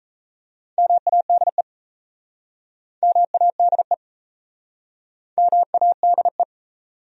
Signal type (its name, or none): Morse code